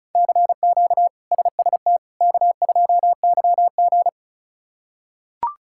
Morse code, heard as a keyed tone, about 36 wpm, 700 Hz. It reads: CQ SST K2YG
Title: K1USN SST QSO Practice - 35wpm